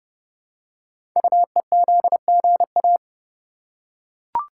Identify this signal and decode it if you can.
Morse code, heard as a keyed tone, about 30 wpm, 700 Hz. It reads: UEZGA
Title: Code Groups - 5 Letter:Numbers - 30wpm